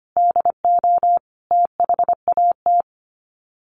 Morse code, heard as a keyed tone, about 25 wpm, 700 Hz. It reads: DO THAT
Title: Sets of 5 Words 25wpm